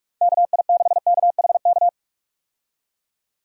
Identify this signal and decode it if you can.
Morse code, heard as a keyed tone, about 48 wpm, 700 Hz. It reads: KI6KHK
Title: USA Call Signs - 6 Characters - 45wpm